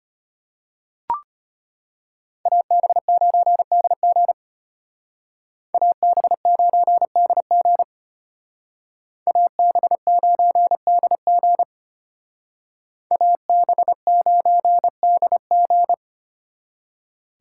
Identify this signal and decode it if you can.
Morse code, heard as a keyed tone, about 31 wpm, 700 Hz. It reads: AB9DG AB9DG AB9DG AB9DG
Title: Speed Racing - USA 2x2 Call Signs 25wpm